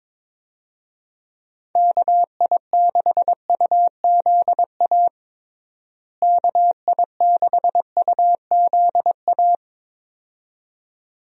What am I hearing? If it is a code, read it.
Morse code, heard as a keyed tone, about 22 wpm, 700 Hz. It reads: KI6UZA KI6UZA
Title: USA Call Signs - 6 Characters - Repeated 2x - 22wpm